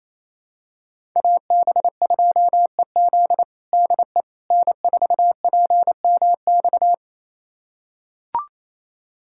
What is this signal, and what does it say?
Morse code, heard as a keyed tone, about 28 wpm, 700 Hz. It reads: AB2EZ DE N4PMX
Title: USA Callsign DE Callsign - 5 Characters - 28wpm